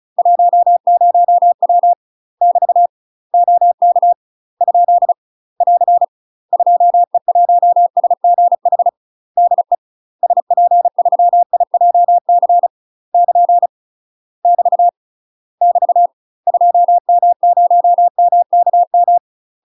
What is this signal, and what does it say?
Morse code, heard as a keyed tone, about 34 wpm, 700 Hz. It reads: 10W <BT> OK ? <AR> 2E1SGH DE SP3IJC <KN> <BT> <BT> 2M0MKM
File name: UK MASTER QSOs TRAINING/35WPM